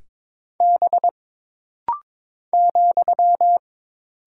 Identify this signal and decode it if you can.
Morse code, heard as a keyed tone, about 22 wpm, 700 Hz. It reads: B ,
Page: This rapid-fire timing is geared to push your ICR (Instant Character Recognition) skills.